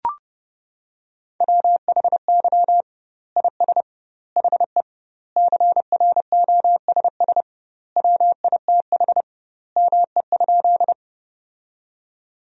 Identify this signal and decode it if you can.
Morse code, heard as a keyed tone, about 30 wpm, 700 Hz. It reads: WHY IS HE CROSS WITH ME?
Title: 2x Word-Spacing - Sentences from Top 300 Words - 30wpm